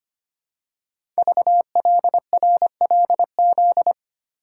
Morse code, heard as a keyed tone, about 25 wpm, 700 Hz. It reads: VLRLZ
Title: Code Groups - 5 Letters - 25wpm